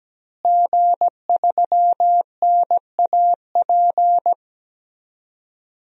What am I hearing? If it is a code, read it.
Morse code, heard as a keyed tone, about 17 wpm, 700 Hz. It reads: G3NAP
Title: DXCC-100 Call Signs - 5 Characters - Repeated 3x - 17wpm